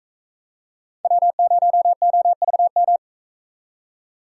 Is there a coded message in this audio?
Yes: Morse code from 1.0 to 3.0 s, heard as a keyed tone, about 44 wpm, 700 Hz. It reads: W0OVM